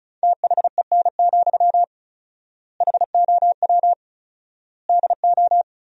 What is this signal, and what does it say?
Morse code, heard as a keyed tone, about 36 wpm, 700 Hz. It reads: THEN, HOW DO
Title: 4x Word-Spacing - Sentences from Top 100 Words - 35wpm